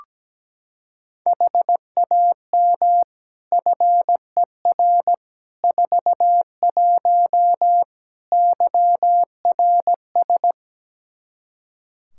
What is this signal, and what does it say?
Morse code, heard as a keyed tone, about 17 wpm, 700 Hz. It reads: HAM FER 41 YRS
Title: Practice QSO - Exchange 3 - More Info - 17wpm